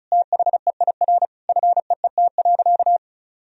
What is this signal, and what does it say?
Morse code, heard as a keyed tone, about 34 wpm, 700 Hz. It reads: THEIR FEET.
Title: Sentences from Top 1000 Words - Review Part 4 - 35wpm